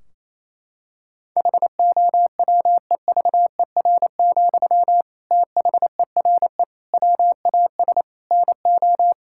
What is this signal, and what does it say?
Morse code, heard as a keyed tone, about 28 wpm, 700 Hz. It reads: HOWEVER, THERE WAS NO
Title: Sentences from Top 1600 Words 28wpm